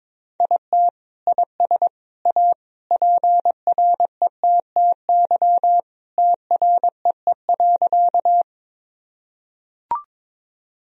Morse code, heard as a keyed tone, about 21 wpm, 700 Hz. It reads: IT IS A PRETTY TREE.
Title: Sentences from Top 1000 Words - Review Part 1 - 22wpm